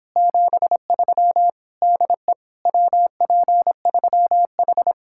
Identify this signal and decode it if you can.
Morse code, heard as a keyed tone, about 26 wpm, 700 Hz. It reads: 73 DE WP3H